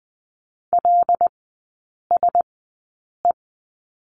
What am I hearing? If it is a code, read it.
Morse code, heard as a keyed tone, about 20 wpm, spaced as for 8 wpm, 700 Hz. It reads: LSE
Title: 114 Lesson 19e - Letter w - Three Characters 20wpm